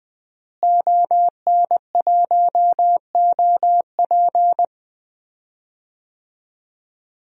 Morse code, heard as a keyed tone, about 20 wpm, 700 Hz. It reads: ON1OP
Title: DXCC-100 Call Signs - 5 Characters - Repeated 2x - 20wpm